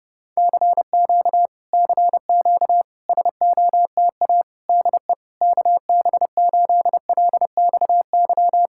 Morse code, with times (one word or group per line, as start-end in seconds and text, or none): 0.4-1.5 CQ
1.7-2.8 CQ
3.1-4.4 SOTA
4.7-5.1 DE
5.4-8.7 KB8LXY